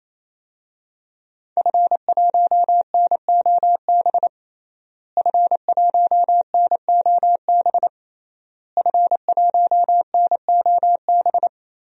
Morse code, heard as a keyed tone, about 27 wpm, 700 Hz. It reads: F1NOB F1NOB F1NOB
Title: DXCC-100 Call Signs - 5 Characters - Repeated 3x - 28wpm